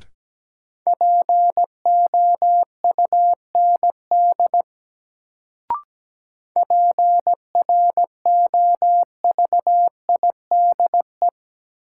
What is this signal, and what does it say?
Morse code, heard as a keyed tone, about 17 wpm, 700 Hz. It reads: POUND PROVIDE
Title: Sentences from Top 1000 Words - Review Part 2 - New Word Drill - 17wpm